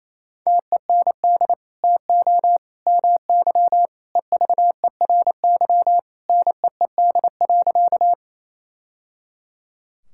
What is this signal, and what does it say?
Morse code, heard as a keyed tone, about 28 wpm, 700 Hz. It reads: TEND TO MY EVERY NEED.
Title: Sentences from Top 1000 Words - Review Part 5 - 28wpm